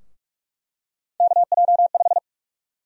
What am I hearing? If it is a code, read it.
Morse code, heard as a keyed tone, about 48 wpm, 700 Hz. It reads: KJ5
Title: USA Call Sign Prefixes - 45wpm